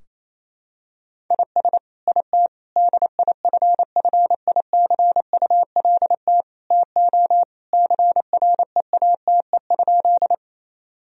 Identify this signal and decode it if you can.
Morse code, heard as a keyed tone, about 27 wpm, 690 Hz. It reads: IS IT DIFFICULT TO CREATE?
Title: Sentences from Top 1000 Words - Review Part 3 - 28wpm